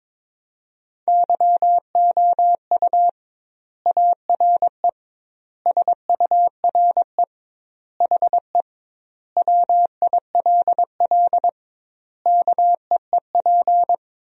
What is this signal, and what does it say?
Morse code, heard as a keyed tone, about 22 wpm, 700 Hz. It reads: YOU ARE SURE HE WILL KEEP
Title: 2x Word-Spacing - Sentences from Top 300 Words - 22wpm